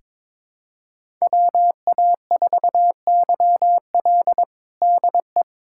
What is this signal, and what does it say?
Morse code, heard as a keyed tone, about 22 wpm, 700 Hz. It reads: WA4YL DE
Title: USA Callsign DE Callsign 22wpm